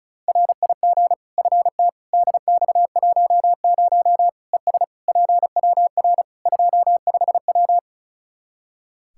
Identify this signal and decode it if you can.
Morse code, heard as a keyed tone, about 34 wpm, 700 Hz. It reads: RIG FT DX10 ES PWR 25W
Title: Practice QSO - Exchange 2 - Info - 35wpm